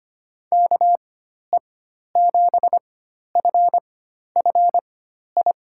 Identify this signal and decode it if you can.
Morse code, heard as a keyed tone, about 25 wpm, spaced as for 12 wpm, 700 Hz. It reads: KE7FFI
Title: USA Call Signs - 6 Characters - 4x Character Spacing - 25wpm